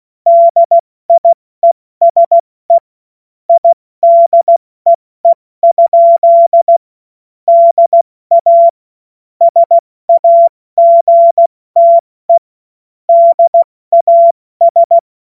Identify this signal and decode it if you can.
Morse code, heard as a keyed tone, about 16 wpm, 680 Hz. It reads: DIESE IDEE? DA SAGTE DAS
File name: Morsekurs / Witze / Witz 23. August 2022 70 F80